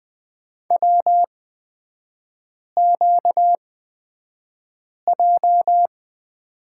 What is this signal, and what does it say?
Morse code, heard as a keyed tone, about 20 wpm, spaced as for 5 wpm, 700 Hz. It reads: WQJ
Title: 236 Lesson 38e - Letter q - Three Characters 20wpm